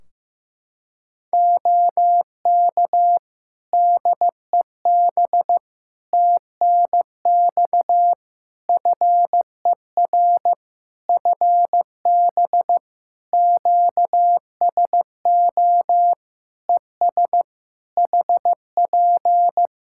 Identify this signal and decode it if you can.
Morse code, heard as a keyed tone, about 15 wpm, 700 Hz. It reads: OK DEB TNX FER FB QSO ES HP
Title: Practice QSO - Exchange 4 - Ending - 15wpm